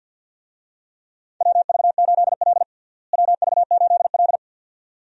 Morse code, heard as a keyed tone, about 52 wpm, 700 Hz. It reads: WV8L WV8L
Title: USA Call Signs - 4 Characters - Repeated 2x - 50wpm